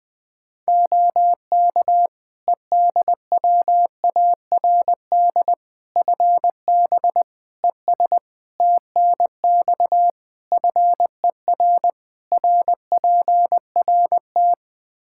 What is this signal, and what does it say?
Morse code, heard as a keyed tone, about 21 wpm, 700 Hz. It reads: OK EDWARD FB ES TNX FER RPRT
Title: Practice QSO - Exchange 2 - Info - 20wpm